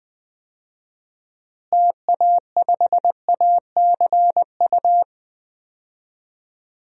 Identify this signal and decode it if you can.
Morse code, heard as a keyed tone, about 20 wpm, 700 Hz. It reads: TA5ACU
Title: DXCC-100 Call Signs - 6 Characters - Repeated 2x - 20wpm